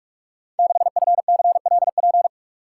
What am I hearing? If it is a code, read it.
Morse code, heard as a keyed tone, about 44 wpm, 700 Hz. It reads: 6FCLP
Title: Code Groups - 5 Letter:Numbers - 45wpm